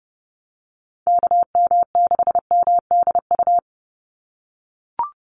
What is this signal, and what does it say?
Morse code, heard as a keyed tone, about 30 wpm, 700 Hz. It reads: KM6MDU
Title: USA Call Signs 30wpm